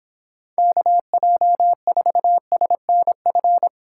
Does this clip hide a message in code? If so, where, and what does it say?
Yes: Morse code from 0.6 to 3.7 s, heard as a keyed tone, about 26 wpm, 700 Hz. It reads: KJ4SNF